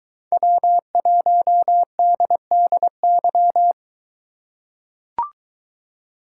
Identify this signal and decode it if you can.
Morse code, heard as a keyed tone, about 22 wpm, 700 Hz. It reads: W1DDY